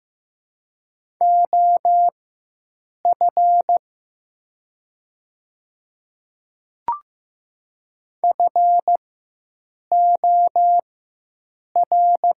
Morse code, heard as a keyed tone, about 15 wpm, spaced as for 7 wpm, 700 Hz. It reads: OF FOR
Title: 4x Character Spacing - Top 100 Words - 15wpm